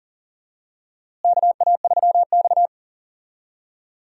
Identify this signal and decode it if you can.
Morse code, heard as a keyed tone, about 40 wpm, 700 Hz. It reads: KA3X
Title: USA Call Signs - 4 Characters - Repeated 3x - 40wpm